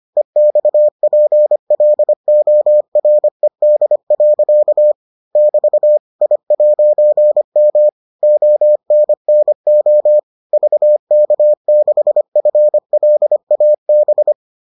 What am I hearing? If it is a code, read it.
Morse code, heard as a keyed tone, about 25 wpm, 600 Hz. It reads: EXPLORED. <BT> I'M ONNO VK6FLAB